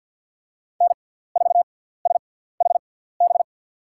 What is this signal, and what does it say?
Morse code, heard as a keyed tone, about 53 wpm, spaced as for 17 wpm, 700 Hz. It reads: N4SHB
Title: USA Call Signs - 5 Characters - 6x Character Spacing - 50wpm